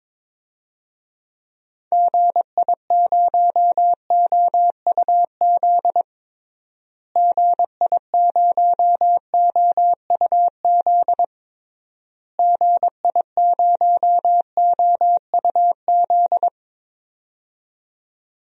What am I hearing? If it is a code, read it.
Morse code, heard as a keyed tone, about 21 wpm, 700 Hz. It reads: GI0OUZ GI0OUZ GI0OUZ
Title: DXCC-100 Call Signs - 6 Characters - Repeated 3x - 22wpm